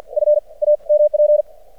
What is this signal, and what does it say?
Morse code, heard as a keyed tone, about 53 wpm, spaced as for 36 wpm, 600 Hz. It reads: FAMP
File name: archived / CRSnet